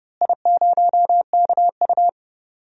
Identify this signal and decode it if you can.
Morse code, heard as a keyed tone, about 31 wpm, 700 Hz. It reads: I0KU
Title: DXCC-100 Call Signs - 4 Characters - Repeated 3x - 30wpm